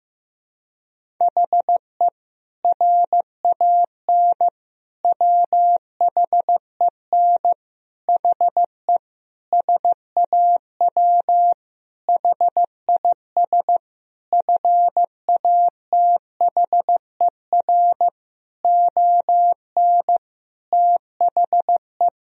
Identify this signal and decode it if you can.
Morse code, heard as a keyed tone, about 15 wpm, 700 Hz. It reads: HE RAN WHEN HE SAW HIS FATHER ON THE
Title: Sentences from Top 500 Words and Abbreviations 15wpm